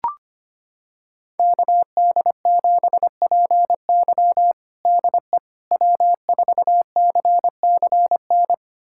Morse code, heard as a keyed tone, about 25 wpm, 700 Hz. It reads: KD7PY DE W4CCN
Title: USA Callsign DE Callsign - 5 Characters - 25wpm